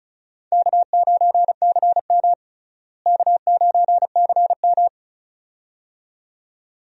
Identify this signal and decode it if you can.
Morse code, heard as a keyed tone, about 34 wpm, 700 Hz. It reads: K9CM K9CM
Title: USA Call Signs - 4 Characters - Repeated 2x - 35wpm